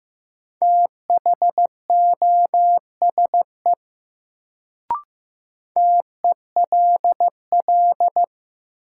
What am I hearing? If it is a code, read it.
Morse code, heard as a keyed tone, about 15 wpm, 700 Hz. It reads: THOSE TELL
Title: Sets of 2 Words - New Word Drill - 15wpm